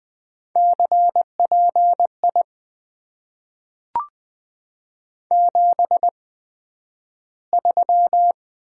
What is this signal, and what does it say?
Morse code, heard as a keyed tone, about 20 wpm, spaced as for 11 wpm, 700 Hz. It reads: CPI 73
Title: Top 90 QSO Elements - 8x Character Spacing - 20wpm